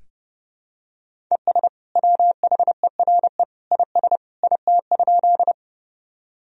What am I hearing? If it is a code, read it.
Morse code, heard as a keyed tone, about 30 wpm, 700 Hz. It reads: ES WHERE IS IT?
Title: Sentences from Top 200 Words and Abbreviations 30wpm